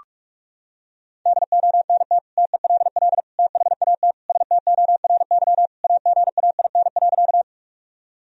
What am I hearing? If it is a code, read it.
Morse code, heard as a keyed tone, about 46 wpm, 700 Hz. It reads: DONT TELL THAT STORY AGAIN.
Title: Sentences from Top 1000 Words - Review Part 1 - 45wpm